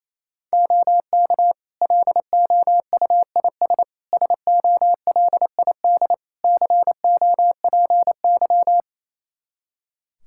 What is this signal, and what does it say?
Morse code, heard as a keyed tone, about 28 wpm, 700 Hz. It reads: OK LOUIS SOLID COPY
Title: Practice QSO - Exchange 3 - More Info - 28wpm